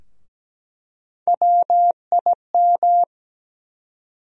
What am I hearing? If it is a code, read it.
Morse code, heard as a keyed tone, about 17 wpm, 700 Hz. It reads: WIM
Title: USA Call Sign Postfixes - 3 Letters 17wpm